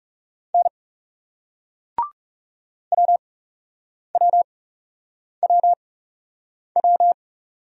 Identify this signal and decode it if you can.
Morse code, heard as a keyed tone, about 41 wpm, spaced as for 8 wpm, 700 Hz. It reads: N WWWW